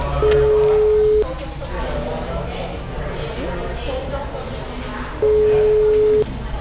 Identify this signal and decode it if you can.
Ringback tone (Europe, 425 Hz), with speech.